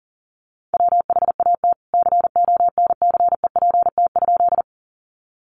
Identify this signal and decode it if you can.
Morse code, heard as a keyed tone, about 39 wpm, 700 Hz. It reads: WHAT CONCEPT?
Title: Sentences from Top 1100 Words 40wpm